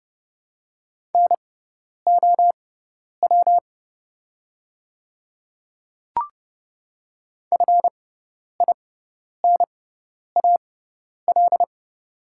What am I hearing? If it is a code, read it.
Morse code, heard as a keyed tone, about 30 wpm, spaced as for 10 wpm, 700 Hz. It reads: NOW FINAL
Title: 6x Character Spacing - Top 500 Words - 30wpm